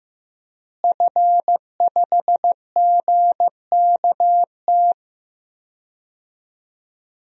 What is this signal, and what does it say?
Morse code, heard as a keyed tone, about 15 wpm, 700 Hz. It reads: F5GKT
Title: DXCC-100 Call Signs - 5 Characters - Repeated 3x - 15wpm